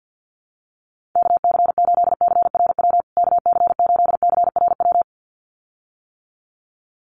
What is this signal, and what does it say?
Morse code, heard as a keyed tone, about 48 wpm, 700 Hz. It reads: KC8CRW KC8CRW
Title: USA Call Signs Repeated 2x 50wpm